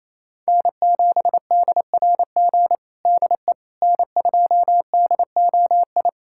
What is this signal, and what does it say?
Morse code, heard as a keyed tone, about 28 wpm, 700 Hz. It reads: N7DRG DE N2DOI
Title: USA Callsign DE Callsign - 5 Characters - 28wpm